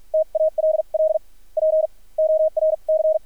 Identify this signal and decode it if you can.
Morse code, heard as a keyed tone, about 50 wpm, spaced as for 35 wpm, 630 Hz, with radio noise. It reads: TALL PORC
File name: archived / CRSnet